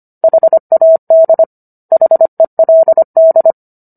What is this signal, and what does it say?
Morse code, heard as a keyed tone, about 25 wpm, 650 Hz. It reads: HAD HELD